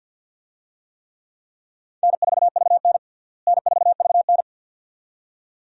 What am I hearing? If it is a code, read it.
Morse code, heard as a keyed tone, about 51 wpm, 700 Hz. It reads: N4VN N4VN
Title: USA Call Signs - 4 Characters - Repeated 2x - 50wpm